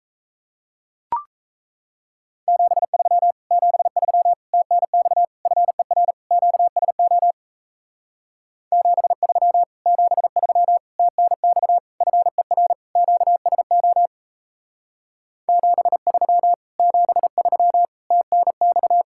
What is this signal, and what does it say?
Morse code, heard as a keyed tone, about 34 wpm, 700 Hz. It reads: 73 73 TNX FER QSO 73 73 TNX FER QSO 73 73 TNX
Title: Speed-Racing Common QSO Phrases Encore - 28wpm